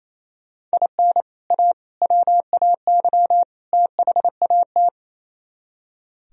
Morse code, heard as a keyed tone, about 34 wpm, 700 Hz. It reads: IN A WAY THAT
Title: Speed-Racing - Sets of 4 Words 28wpm